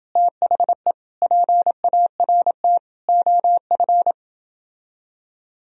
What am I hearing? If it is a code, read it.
Morse code, heard as a keyed tone, about 29 wpm, 700 Hz. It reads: THE PART OF